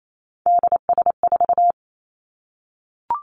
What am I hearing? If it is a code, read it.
Morse code, heard as a keyed tone, about 28 wpm, 700 Hz. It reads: DS4
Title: Three Challenging Letter-Number Combinations 28wpm